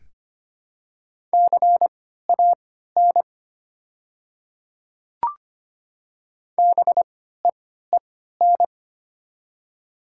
Morse code, heard as a keyed tone, about 25 wpm, spaced as for 14 wpm, 700 Hz. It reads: CAN BEEN
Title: 3x Character Spacing - Top 100 Words - 25wpm